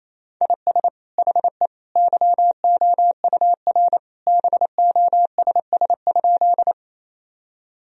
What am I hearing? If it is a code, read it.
Morse code, heard as a keyed tone, about 28 wpm, 700 Hz. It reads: IS HE YOUR BOSS?
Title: Sentences from Top 1000 Words - Review Part 4 - 28wpm